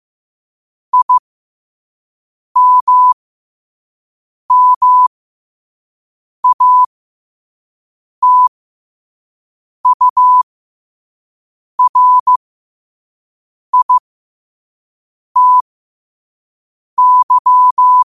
Morse code, heard as a keyed tone, about 15 wpm, spaced as for 5 wpm, 1000 Hz. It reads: IMMATURITY
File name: qotdfiles / 5wpm